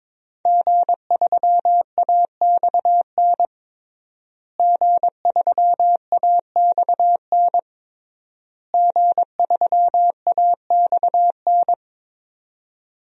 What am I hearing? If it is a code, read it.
Morse code, heard as a keyed tone, about 22 wpm, 700 Hz. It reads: G3AXN G3AXN G3AXN
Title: DXCC-100 Call Signs - 5 Characters - Repeated 3x - 22wpm